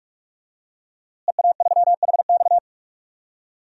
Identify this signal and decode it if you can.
Morse code, heard as a keyed tone, about 46 wpm, 700 Hz. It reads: EA3HX
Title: DXCC-100 Call Signs - 5 Characters - Repeated 3x - 45wpm